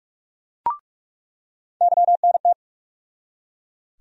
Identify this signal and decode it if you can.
Morse code, heard as a keyed tone, about 42 wpm, 700 Hz. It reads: YNT
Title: USA Call Sign Postfixes - 3 Letters 45wpm